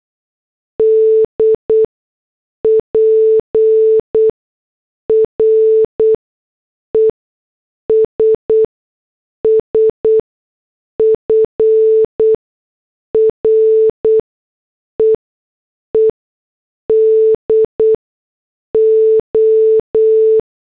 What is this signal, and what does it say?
Morse code, heard as a keyed tone, about 8 wpm, spaced as for 6 wpm, 440 Hz. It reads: D PRESS FREEDO